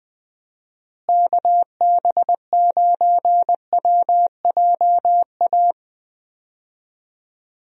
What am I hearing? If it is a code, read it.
Morse code, heard as a keyed tone, about 21 wpm, 700 Hz. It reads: KB9WJA
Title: USA Call Signs - 6 Characters - Repeated 3x - 20wpm